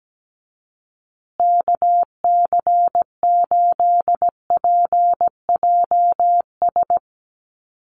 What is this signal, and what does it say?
Morse code, heard as a keyed tone, about 17 wpm, 700 Hz. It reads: KC8PJS
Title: USA Call Signs Repeated 3x 17wpm